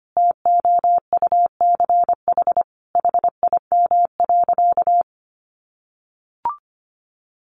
Morse code, heard as a keyed tone, about 25 wpm, 700 Hz. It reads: TOUCH HIM.
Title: Sentences from Top 1200 Words 25wpm